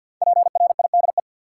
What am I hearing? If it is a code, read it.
Morse code, heard as a keyed tone, about 52 wpm, 700 Hz. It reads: PRIDE
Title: Five Letter Words - 50wpm